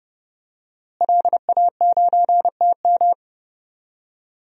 Morse code, heard as a keyed tone, about 30 wpm, 700 Hz. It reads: LA9TM
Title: DXCC-100 Call Signs - 5 Characters - Repeated 2x - 30wpm